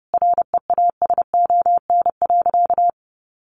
Morse code, heard as a keyed tone, about 30 wpm, 700 Hz. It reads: REASON.
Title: Sentences from Top 1400 Words 30wpm